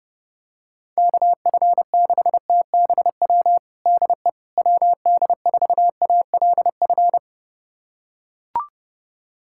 Morse code, heard as a keyed tone, about 31 wpm, 700 Hz. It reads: KF6TBW DE WD4ALF
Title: USA Callsign DE Callsign - 6 Characters - 30wpm